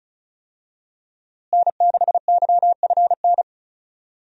Morse code, heard as a keyed tone, about 36 wpm, 700 Hz. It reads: N6YFN
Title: USA Call Signs - 5 Characters - Repeated 3x - 35wpm